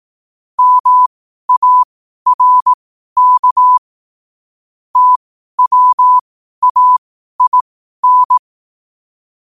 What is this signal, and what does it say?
Morse code, heard as a keyed tone, about 18 wpm, spaced as for 12 wpm, 1000 Hz. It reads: MARK TWAIN
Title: 12 WPM morse code quotes for Thu, 29 Feb 2024